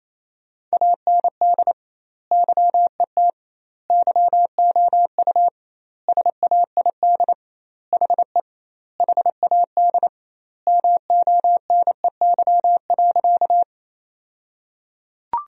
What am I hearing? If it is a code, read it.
Morse code, heard as a keyed tone, about 28 wpm, 700 Hz. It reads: AND YET YOU SAID HE HAD MONEY.
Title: 2x Word-Spacing - Sentences from Top 500 Words - 28wpm